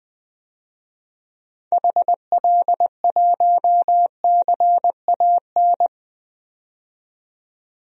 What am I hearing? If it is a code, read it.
Morse code, heard as a keyed tone, about 20 wpm, 700 Hz. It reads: HL1CAN
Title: DXCC-100 Call Signs - 6 Characters - 20wpm